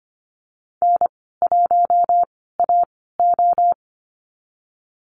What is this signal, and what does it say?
Morse code, heard as a keyed tone, about 25 wpm, spaced as for 16 wpm, 700 Hz. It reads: N1AO
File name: Lesson 15g - Number 5 - Call Signs 25wpm